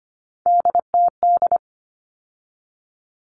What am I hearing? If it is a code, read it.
Morse code, heard as a keyed tone, about 25 wpm, 700 Hz. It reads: DTD
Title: All Cut Numbers - 3-Digit 25wpm